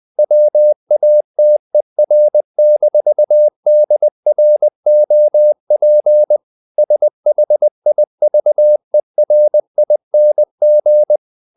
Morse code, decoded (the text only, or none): WATER-DROP SHIVERING